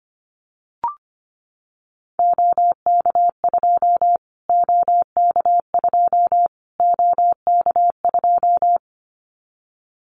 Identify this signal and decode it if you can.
Morse code, heard as a keyed tone, about 25 wpm, 700 Hz. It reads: OK2 OK2 OK2
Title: Callsign Prefixes for Top 25 DXCC - Repeated 3x - 25wpm